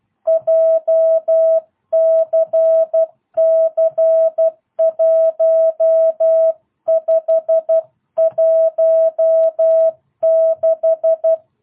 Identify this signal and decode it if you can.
Morse code, heard as a keyed tone, about 12 wpm, 660 Hz. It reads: JCC1516